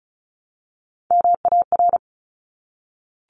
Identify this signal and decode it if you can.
Morse code, heard as a keyed tone, about 36 wpm, 700 Hz. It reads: MAR
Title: Top Three Letter Combinations in Words 35wpm